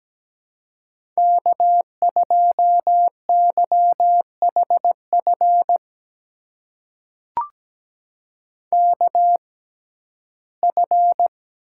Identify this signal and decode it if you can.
Morse code, heard as a keyed tone, about 17 wpm, spaced as for 12 wpm, 700 Hz. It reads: K2YHF KF
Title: USA Call Signs - 5 Characters - 6x Character Spacing - 17wpm